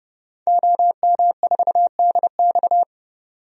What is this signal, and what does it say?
Morse code, heard as a keyed tone, about 32 wpm, 700 Hz. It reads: OM4DX
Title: DXCC-100 Call Signs - 5 Characters - 30wpm